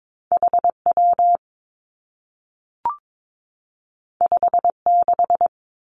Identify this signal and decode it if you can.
Morse code, heard as a keyed tone, about 22 wpm, 700 Hz. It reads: H W 5 6
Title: Two Challenging Letter-Number Combinations 22wpm